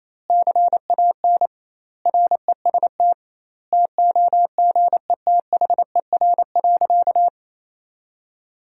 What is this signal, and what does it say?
Morse code, heard as a keyed tone, about 28 wpm, 700 Hz. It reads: CAN REST TOGETHER.
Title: 2x Word-Spacing - Sentences from Top 500 Words - 28wpm